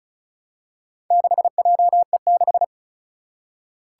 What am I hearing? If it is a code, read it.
Morse code, heard as a keyed tone, about 34 wpm, 700 Hz. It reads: 6JE6
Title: Four Challenging Letter-Number Combinations 35wpm